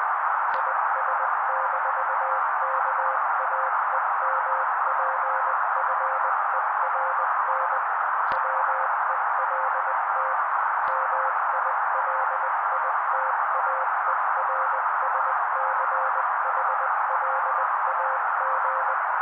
Morse code, heard as a keyed tone, about 20 wpm, 530 Hz. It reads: IS-KAEMPFERN WELT MILITAERSCHLAG
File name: daily / 18wpm noise